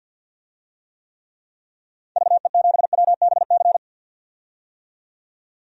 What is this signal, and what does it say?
Morse code, heard as a keyed tone, about 50 wpm, 700 Hz. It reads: VE7WBC
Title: DXCC-100 Call Signs - 6 Characters - 50wpm